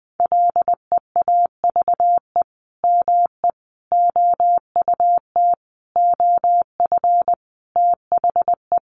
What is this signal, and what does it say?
Morse code, heard as a keyed tone, about 20 wpm, 700 Hz. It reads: LEAVE ME OUT OF THE
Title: Sentences from Top 400 Words 20wpm